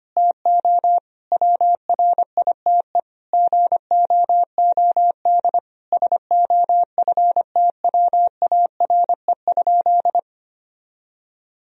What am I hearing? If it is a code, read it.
Morse code, heard as a keyed tone, about 25 wpm, 700 Hz. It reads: TO WRITE GOOD SOFTWARE?
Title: Sentences from Top 1000 Words - Review Part 3 - 25wpm